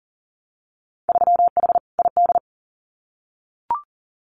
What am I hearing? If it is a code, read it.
Morse code, heard as a keyed tone, about 38 wpm, 700 Hz. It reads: 3H ID
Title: Winter Field Day Exchanges Encore 40wpm